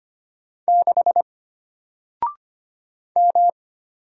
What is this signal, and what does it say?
Morse code, heard as a keyed tone, about 25 wpm, 700 Hz. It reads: 6 M
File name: Single-Letter-Number-25wpm